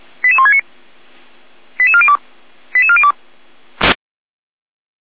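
Five-tone selective call (ZVEI), heard as quick runs of five tones, tones of about 70 ms. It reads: 88169 88402 88402
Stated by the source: ZVEI five-tone call 88166 received by SDR.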